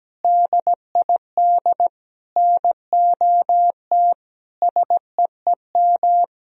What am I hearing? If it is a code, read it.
Morse code, heard as a keyed tone, about 17 wpm, 700 Hz. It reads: DID NOT SEEM
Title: Sentences from Top 1000 Words - Review Part 1 - 17wpm